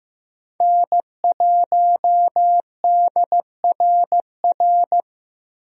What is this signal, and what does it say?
Morse code, heard as a keyed tone, about 15 wpm, 700 Hz. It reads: N1DRR
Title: USA Call Signs - 5 Characters - 15wpm